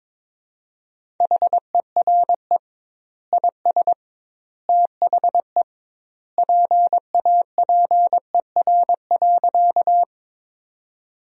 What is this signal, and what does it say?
Morse code, heard as a keyed tone, about 22 wpm, 700 Hz. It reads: HERE IS THE PAPER.
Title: 2x Word-Spacing - Sentences from Top 300 Words - 22wpm